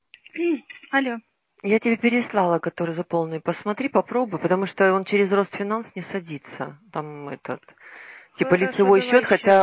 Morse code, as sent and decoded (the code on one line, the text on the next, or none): none